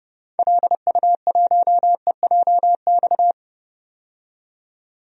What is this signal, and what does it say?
Morse code, heard as a keyed tone, about 30 wpm, 700 Hz. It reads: LU1EJX
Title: DXCC-100 Call Signs - 6 Characters - 30wpm